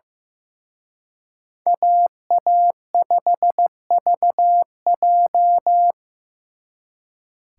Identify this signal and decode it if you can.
Morse code, heard as a keyed tone, about 18 wpm, 700 Hz. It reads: AA5VJ
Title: Speed Racing - USA 2x2 Call Signs 15wpm